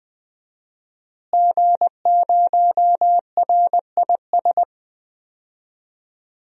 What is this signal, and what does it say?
Morse code, heard as a keyed tone, about 20 wpm, 700 Hz. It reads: G0RIS
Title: DXCC-100 Call Signs - 5 Characters - Repeated 3x - 20wpm